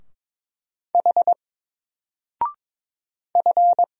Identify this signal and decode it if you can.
Morse code, heard as a keyed tone, about 22 wpm, 700 Hz. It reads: H F
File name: Single-Letter-Number-22wpm